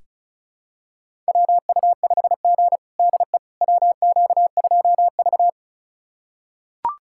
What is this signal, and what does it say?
Morse code, heard as a keyed tone, about 36 wpm, 700 Hz. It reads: WU5G DE WQ2V
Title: USA Callsign DE Callsign - 4 Characters - 35wpm